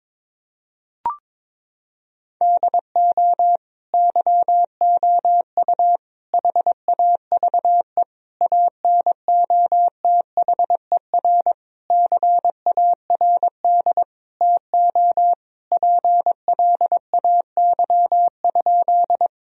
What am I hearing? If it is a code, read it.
Morse code, heard as a keyed tone, about 21 wpm, 700 Hz. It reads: DO YOU HAVE ANOTHER CARD TO PLAY?
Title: Sentences from Top 1000 Words - Review Part 4 - 22wpm